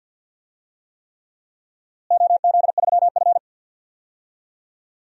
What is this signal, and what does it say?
Morse code, heard as a keyed tone, about 53 wpm, 700 Hz. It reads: OZ3F
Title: DXCC-100 Call Signs - 4 Characters - Repeated 3x - 50wpm